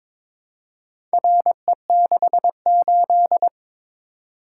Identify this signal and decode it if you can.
Morse code, heard as a keyed tone, about 21 wpm, 700 Hz. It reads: RE68
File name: practice / 22wpm / Four Letter-Number Combinations - 22wpm